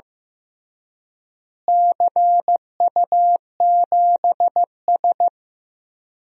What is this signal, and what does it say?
Morse code, heard as a keyed tone, about 15 wpm, 700 Hz. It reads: CU7S
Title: DXCC-100 Call Signs - 4 Characters - Repeated 2x - 15wpm